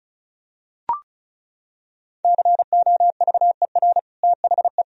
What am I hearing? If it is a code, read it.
Morse code, heard as a keyed tone, about 36 wpm, 700 Hz. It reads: COVER THE
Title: Sentences from Top 1000 Words - Review Part 1 - 35wpm